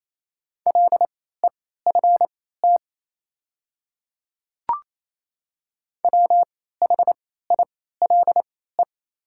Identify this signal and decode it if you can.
Morse code, heard as a keyed tone, about 28 wpm, spaced as for 16 wpm, 700 Hz. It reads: LEFT WHILE
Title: 3x Character Spacing - Top 300 Words - 28wpm